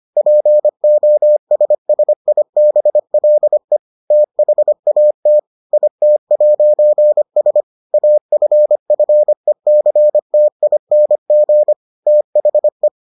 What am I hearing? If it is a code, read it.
Morse code, heard as a keyed tone, about 25 wpm, 600 Hz. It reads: POSSIBLE THAT IT'S AFFECTING THE